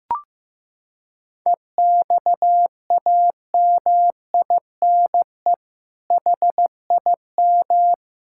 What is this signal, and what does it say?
Morse code, heard as a keyed tone, about 15 wpm, 700 Hz. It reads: EXAMINE HIM
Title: Sentences from Top 1600 Words 15wpm